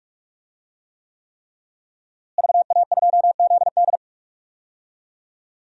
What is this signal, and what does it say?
Morse code, heard as a keyed tone, about 46 wpm, 700 Hz. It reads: VA2ZD
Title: DXCC-100 Call Signs - 5 Characters - Repeated 3x - 45wpm